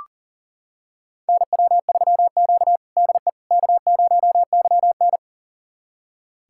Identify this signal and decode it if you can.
Morse code, heard as a keyed tone, about 40 wpm, 700 Hz. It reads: NW3Q DE K0YN
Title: USA Callsign DE Callsign - 4 Characters - 40wpm